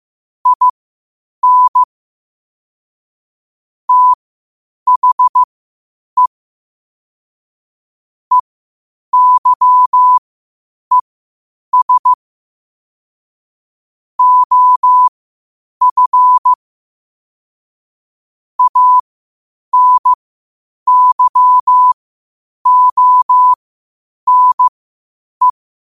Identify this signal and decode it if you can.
Morse code, heard as a keyed tone, about 15 wpm, spaced as for 8 wpm, 1000 Hz. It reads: IN THE EYES OF ANYONE
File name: qotdfiles / 8wpm